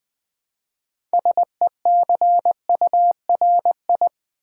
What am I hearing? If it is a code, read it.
Morse code, heard as a keyed tone, about 20 wpm, 700 Hz. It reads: SECURI
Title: Build A Word - 2 Letters At A Time - 8-Letter Words - 20wpm